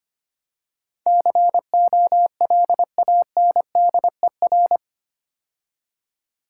Morse code, heard as a keyed tone, about 28 wpm, 700 Hz. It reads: COLANDER
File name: Easy Words - Speed-Racing with Hints - 25wpm